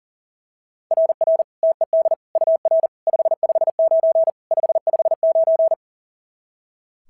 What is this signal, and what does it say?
Morse code, heard as a keyed tone, about 37 wpm, 690 Hz. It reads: RR TED UR 559 559
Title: SOTA QSO Practice 40wpm